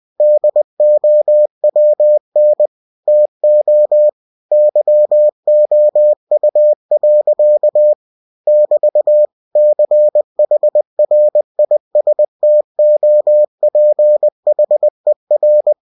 Morse code, heard as a keyed tone, about 20 wpm, 600 Hz. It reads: DOWN TO YOU. <BT> CHRISTOPHER
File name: cw-books / winnie-the-pooh / winnie-20wpm-page-0007